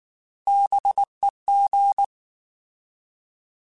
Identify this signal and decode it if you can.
Morse code, heard as a keyed tone, about 19 wpm, 800 Hz. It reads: BEG
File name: Alphabetical / Alphabetical mp3s / Bs @18